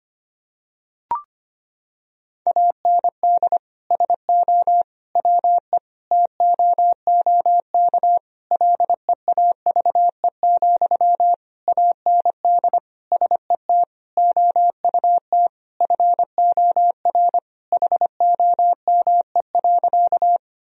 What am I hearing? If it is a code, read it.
Morse code, heard as a keyed tone, about 25 wpm, 700 Hz. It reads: AND SO WE TOOK LEAVE, AND SET OUT FOR HOME.
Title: Sentences from Top 1000 Words - Review Part 1 - 25wpm